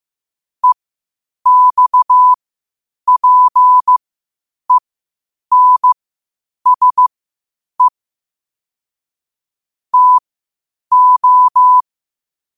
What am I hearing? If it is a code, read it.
Morse code, heard as a keyed tone, about 15 wpm, spaced as for 8 wpm, 1000 Hz. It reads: EXPENSE TO
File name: qotdfiles / 8wpm